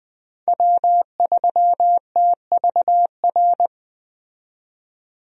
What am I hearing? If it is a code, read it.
Morse code, heard as a keyed tone, about 20 wpm, 700 Hz. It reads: W3TVR
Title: USA Call Signs - 5 Characters - 20wpm